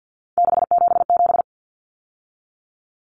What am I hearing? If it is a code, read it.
Morse code, heard as a keyed tone, about 50 wpm, 700 Hz. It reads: 677
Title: Numbers - 3-Digit 50wpm